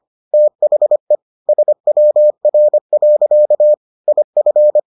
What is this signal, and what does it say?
Morse code, heard as a keyed tone, about 25 wpm, 600 Hz. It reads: THE SWR. IF